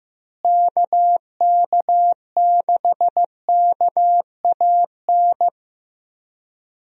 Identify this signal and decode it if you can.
Morse code, heard as a keyed tone, about 16 wpm, 700 Hz. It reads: KK6KAN
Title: USA Call Signs - 6 Characters - Repeated 2x - 15wpm